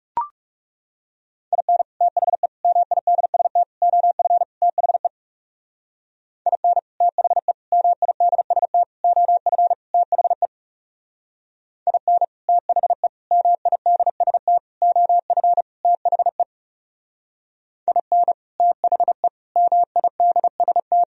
Morse code, heard as a keyed tone, about 37 wpm, 700 Hz. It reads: IN THE MIDST OF THE IN THE MIDST OF THE IN THE MIDST OF THE IN THE MIDST
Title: Speed-Racing - Sets of 5 Words 30wpm